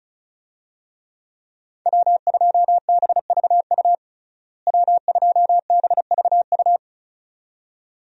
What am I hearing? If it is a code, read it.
Morse code, heard as a keyed tone, about 36 wpm, 700 Hz. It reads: W2BVU W2BVU
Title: USA Call Signs - 5 Characters - Repeated 2x - 35wpm